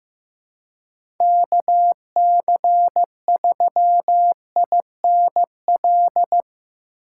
Morse code, heard as a keyed tone, about 15 wpm, 700 Hz. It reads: KC3INL
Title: USA Call Signs - 6 Characters - 2x Character Spacing - 15wpm